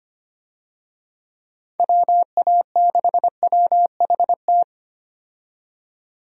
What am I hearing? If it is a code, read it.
Morse code, heard as a keyed tone, about 25 wpm, 700 Hz. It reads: WA6WHT
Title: USA Call Signs - 6 Characters - 25wpm